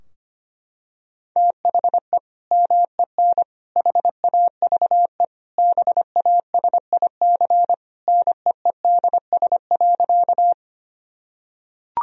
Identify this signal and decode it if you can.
Morse code, heard as a keyed tone, about 25 wpm, 700 Hz. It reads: THE MEN HAVE BASIC NEEDS.
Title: Sentences from Top 1000 Words - Review Part 3 - 25wpm